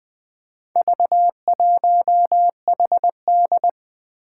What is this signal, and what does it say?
Morse code, heard as a keyed tone, about 20 wpm, 700 Hz. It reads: V1HD
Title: Four Challenging Letter-Number Combinations 20wpm